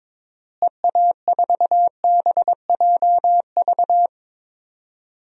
Morse code, heard as a keyed tone, about 22 wpm, 700 Hz. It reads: EA4BJV
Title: DXCC-100 Call Signs - 6 Characters - Repeated 3x - 22wpm